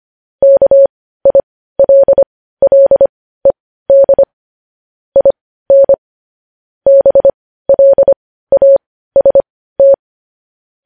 Morse code, heard as a keyed tone, about 25 wpm, spaced as for 15 wpm, 550 Hz. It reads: KILLED IN BLAST
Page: news headlines in morse code at 15 wpm